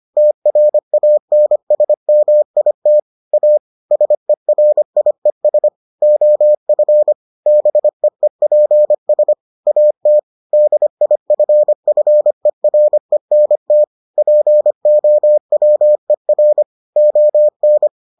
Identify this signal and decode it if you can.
Morse code, heard as a keyed tone, about 25 wpm, 600 Hz. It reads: TRANSMIT A SERIES OF BEEPS AT DIFFERENT POWER ON